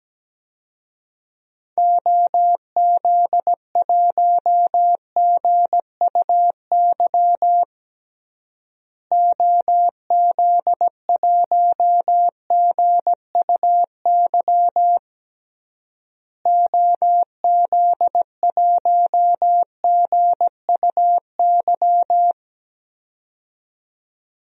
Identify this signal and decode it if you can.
Morse code, heard as a keyed tone, about 17 wpm, 700 Hz. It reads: OZ1GUY OZ1GUY OZ1GUY
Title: DXCC-100 Call Signs - 6 Characters - Repeated 3x - 17wpm